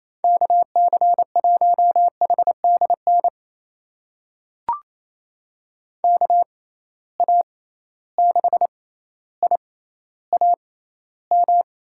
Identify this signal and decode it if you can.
Morse code, heard as a keyed tone, about 27 wpm, spaced as for 14 wpm, 700 Hz. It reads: KC1HDN KA6IAM
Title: USA Call Signs - 6 Characters - 6x Character Spacing - 28wpm